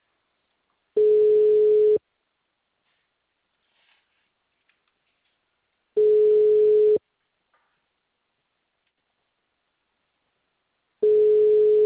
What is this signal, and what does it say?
Ringback tone (Europe, 425 Hz).
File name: out- -2034- - - .7070